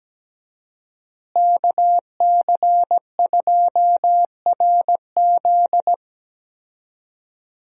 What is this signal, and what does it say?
Morse code, heard as a keyed tone, about 17 wpm, 700 Hz. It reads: KC2RZ
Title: USA Call Signs - 5 Characters - Repeated 2x - 17wpm